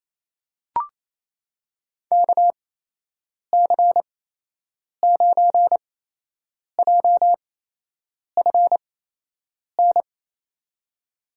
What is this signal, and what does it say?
Morse code, heard as a keyed tone, about 27 wpm, spaced as for 8 wpm, 700 Hz. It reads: KC9JFN
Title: USA Call Signs - 6 Characters - 8x Character Spacing - 28wpm